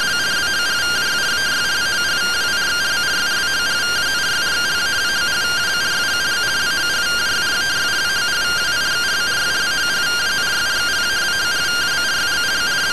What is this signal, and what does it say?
Two-tone teleprinter signal, about 170 Hz shift.